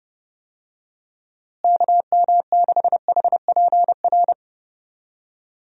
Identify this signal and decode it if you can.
Morse code, heard as a keyed tone, about 31 wpm, 700 Hz. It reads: KM6HPR
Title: USA Call Signs - 6 Characters - Repeated 3x - 30wpm